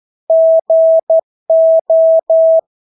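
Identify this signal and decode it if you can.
Morse code, heard as a keyed tone, about 12 wpm, 650 Hz. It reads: GO
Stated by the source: Morse practice of English words at 12 wpm.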